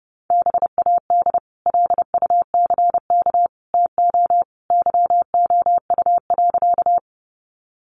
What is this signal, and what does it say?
Morse code, heard as a keyed tone, about 28 wpm, 700 Hz. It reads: BAD LUCK TO YOU.
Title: Sentences from Top 1500 Words 30wpm